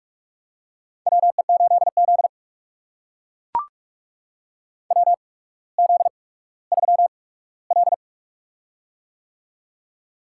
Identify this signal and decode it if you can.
Morse code, heard as a keyed tone, about 44 wpm, spaced as for 19 wpm, 700 Hz. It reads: WE8Z WZ3L
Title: USA Call Signs - 4 Characters - 8x Character Spacing - 45wpm